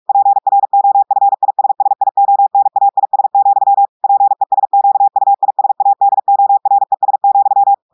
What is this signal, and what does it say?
Morse code, heard as a keyed tone, about 46 wpm, 800 Hz. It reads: PROFISSIONAIS, PESQUISADORES,
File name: portugais texte en clair 1 vitesse 45 mots minutes